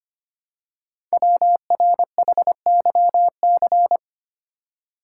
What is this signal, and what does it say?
Morse code, heard as a keyed tone, about 25 wpm, 700 Hz. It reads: WRHYC
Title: Code Groups - 5 Letters - 25wpm